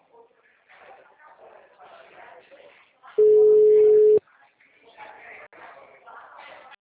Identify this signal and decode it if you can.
Ringback tone (Europe, 425 Hz), with speech.
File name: out- -2004- - - .33897